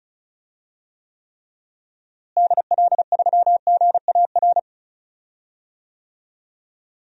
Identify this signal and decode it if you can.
Morse code, heard as a keyed tone, about 36 wpm, 700 Hz. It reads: DL3GAR
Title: DXCC-100 Call Signs - 6 Characters - Repeated 3x - 35wpm